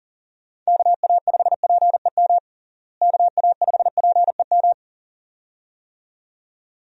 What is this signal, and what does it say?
Morse code, heard as a keyed tone, about 40 wpm, 700 Hz. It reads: KA5PEM KA5PEM
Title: USA Call Signs - 6 Characters - Repeated 2x - 40wpm